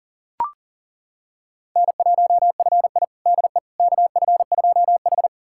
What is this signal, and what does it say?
Morse code, heard as a keyed tone, about 40 wpm, 700 Hz. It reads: N1FI DE KF2H
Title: USA Callsign DE Callsign - 4 Characters - 40wpm